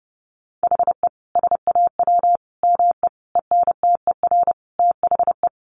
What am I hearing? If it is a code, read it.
Morse code, heard as a keyed tone, about 30 wpm, 700 Hz. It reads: HE SAW ME ENTER THE
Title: Sentences from Top 1300 Words 30wpm